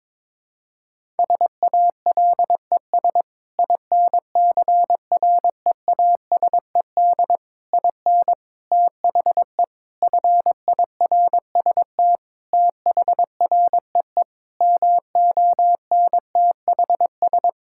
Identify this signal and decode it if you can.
Morse code, heard as a keyed tone, about 22 wpm, 700 Hz. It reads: SALES INCREASED IN THE FIRST THREE MONTHS
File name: practice / 22wpm / Sentences with 8-Word Phrases - 22wpm